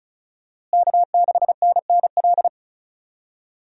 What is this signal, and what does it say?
Morse code, heard as a keyed tone, about 34 wpm, 700 Hz. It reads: K6NNL
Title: USA Call Signs - 5 Characters - Repeated 3x - 35wpm